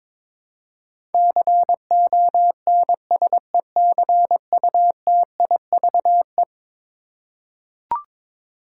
Morse code, heard as a keyed tone, about 22 wpm, 700 Hz. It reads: CONSECUTIVE
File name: practice / 22wpm / Eleven Letter Words - 22wpm